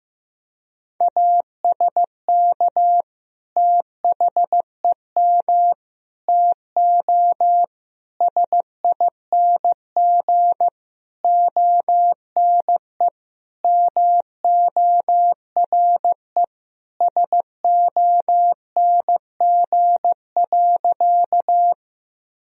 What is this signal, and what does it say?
Morse code, heard as a keyed tone, about 15 wpm, 700 Hz. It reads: ASK THEM TO SING ONE MORE SONG.
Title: Sentences from Top 1000 Words - Review Part 2 - 15wpm